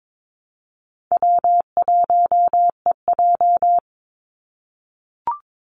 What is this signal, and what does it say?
Morse code, heard as a keyed tone, about 22 wpm, 700 Hz. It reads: W1EJ
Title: NA SOTA Activator Callsigns 22wpm